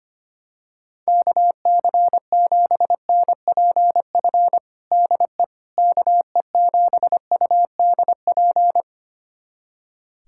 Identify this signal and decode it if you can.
Morse code, heard as a keyed tone, about 25 wpm, 700 Hz. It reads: KC7NPF DE KE7UDP
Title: USA Callsign DE Callsign - 6 Characters - 25wpm